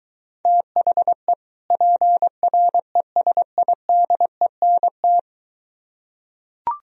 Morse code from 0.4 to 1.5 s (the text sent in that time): THE